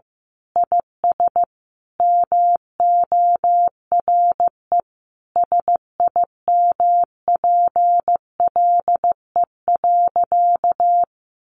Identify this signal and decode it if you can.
Morse code, heard as a keyed tone, about 15 wpm, 700 Hz. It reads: IS MORE SIMPLE.
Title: Sentences from Top 500 Words 15wpm